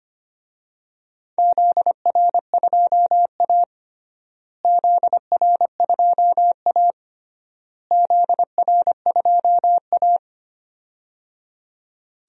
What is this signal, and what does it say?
Morse code, heard as a keyed tone, about 25 wpm, 700 Hz. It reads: ZR2A ZR2A ZR2A
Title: DXCC-100 Call Signs - 4 Characters - Repeated 3x - 25wpm